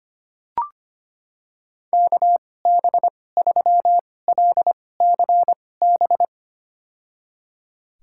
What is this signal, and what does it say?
Morse code, heard as a keyed tone, about 25 wpm, spaced as for 18 wpm, 700 Hz. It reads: KB3LCB
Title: USA Call Signs - 6 Characters - 2x Character Spacing - 25wpm